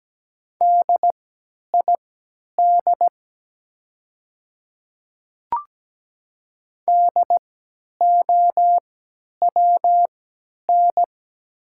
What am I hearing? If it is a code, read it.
Morse code, heard as a keyed tone, about 17 wpm, spaced as for 10 wpm, 700 Hz. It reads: DID DOWN
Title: 3x Character Spacing - Top 300 Words - 17wpm